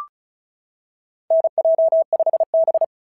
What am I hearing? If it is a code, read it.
Morse code, heard as a keyed tone, about 36 wpm, 690 Hz. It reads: NJ5B
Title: SOTA QSO Practice 35wpm